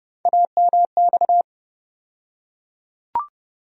Morse code, heard as a keyed tone, about 30 wpm, 700 Hz. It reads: AMX
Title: USA Call Sign Postfixes - 3 Letters 30wpm